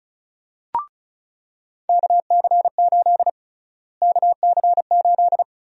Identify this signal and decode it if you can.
Morse code, heard as a keyed tone, about 36 wpm, 700 Hz. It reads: KC8 KC8
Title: USA Call Sign Prefixes - Repeated 2x - 35wpm